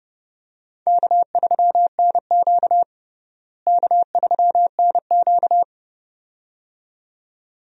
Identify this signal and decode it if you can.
Morse code, heard as a keyed tone, about 30 wpm, 700 Hz. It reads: K3NQ K3NQ
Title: USA Call Signs - 4 Characters - Repeated 2x - 30wpm